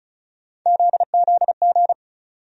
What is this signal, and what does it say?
Morse code, heard as a keyed tone, about 36 wpm, 700 Hz. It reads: ZZG
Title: USA Call Sign Postfixes - 3 Letters 35wpm